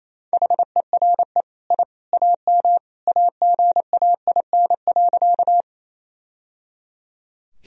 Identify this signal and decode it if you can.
Morse code, heard as a keyed tone, about 28 wpm, 700 Hz. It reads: HERE I AM AGAIN.
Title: Sentences from Top 1000 Words - Review Part 2 - 28wpm